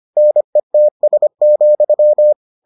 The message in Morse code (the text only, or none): NETS,